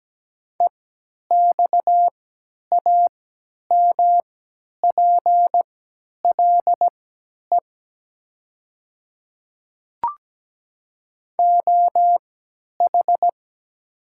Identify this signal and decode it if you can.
Morse code, heard as a keyed tone, about 17 wpm, spaced as for 10 wpm, 700 Hz. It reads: EXAMPLE OH
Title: 3x Character Spacing - Top 500 Words - 17wpm